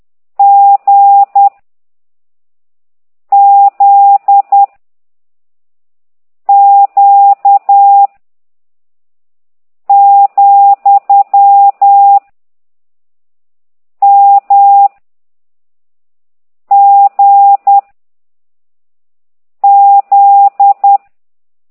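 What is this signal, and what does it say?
Morse code, heard as a keyed tone, about 10 wpm, 800 Hz. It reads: G Z Q , M G Z
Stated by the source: Morse code lesson 5 exercise 10 by CT2IDW, containing only characters M, O, G, Z, Q and comma.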